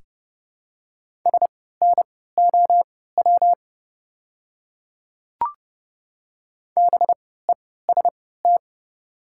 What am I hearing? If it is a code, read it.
Morse code, heard as a keyed tone, about 30 wpm, spaced as for 17 wpm, 700 Hz. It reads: SNOW BEST
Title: 3x Character Spacing - Top 500 Words - 30wpm